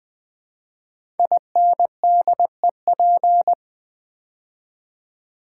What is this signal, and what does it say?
Morse code, heard as a keyed tone, about 20 wpm, 700 Hz. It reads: INDEP
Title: Build A Word - 5 Letters At A Time - 12-Letter Words - 20wpm